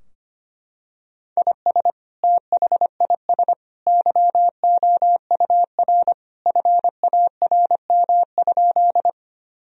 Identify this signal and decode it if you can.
Morse code, heard as a keyed tone, about 25 wpm, 700 Hz. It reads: IS THIS YOUR FARM?
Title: Sentences from Top 1000 Words - Review Part 2 - 25wpm